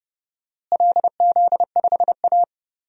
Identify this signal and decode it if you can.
Morse code, heard as a keyed tone, about 30 wpm, 700 Hz. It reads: LZ5A
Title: DXCC-100 Call Signs - 4 Characters - Repeated 2x - 30wpm